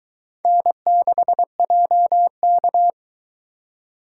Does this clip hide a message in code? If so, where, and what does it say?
Yes: Morse code from 0.5 to 2.9 s, heard as a keyed tone, about 22 wpm, 700 Hz. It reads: N6JK